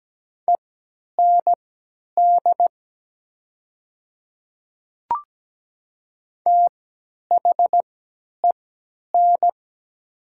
Morse code, heard as a keyed tone, about 17 wpm, spaced as for 10 wpm, 700 Hz. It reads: END THEN
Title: 3x Character Spacing - Top 200 Words - 17wpm